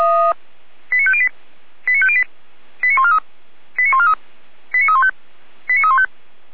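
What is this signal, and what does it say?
Five-tone selective call (ZVEI), heard as quick runs of five tones, tones of about 70 ms. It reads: 89508 89508 89143 89143 89315 89315
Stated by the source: ZVEI five-tone call received via SDR.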